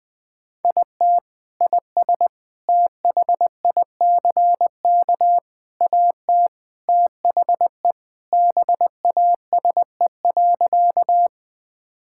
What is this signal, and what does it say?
Morse code, heard as a keyed tone, about 20 wpm, 700 Hz. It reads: IT IS THICK AT THE BASE.
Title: Sentences from Top 1600 Words 20wpm